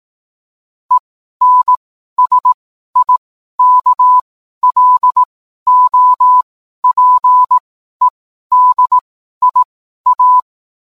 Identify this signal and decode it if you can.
Morse code, heard as a keyed tone, about 18 wpm, spaced as for 12 wpm, 1000 Hz. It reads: ENSIKLOPEDIA